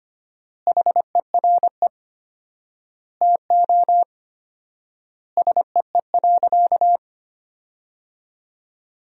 Morse code, heard as a keyed tone, about 25 wpm, 700 Hz. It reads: HERE TO SEE.
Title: 4x Word-Spacing - Sentences from Top 200 Words - 25wpm